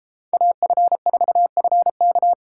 Morse code, heard as a keyed tone, about 33 wpm, 700 Hz. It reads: AF4FK